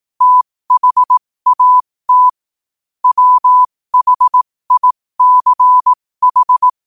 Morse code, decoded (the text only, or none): THAT WHICH